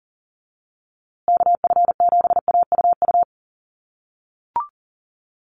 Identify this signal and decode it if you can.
Morse code, heard as a keyed tone, about 40 wpm, 700 Hz. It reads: KF7AUU
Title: USA Call Signs Repeated 3x 40wpm